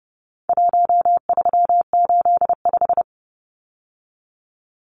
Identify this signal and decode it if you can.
Morse code, heard as a keyed tone, about 30 wpm, 700 Hz. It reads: 1385
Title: Numbers - 4-Digit 30wpm